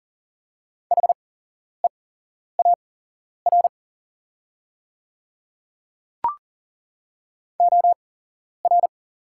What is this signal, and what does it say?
Morse code, heard as a keyed tone, about 38 wpm, spaced as for 11 wpm, 700 Hz. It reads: HEAR OR
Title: 8x Character Spacing - Top 300 Words - 40wpm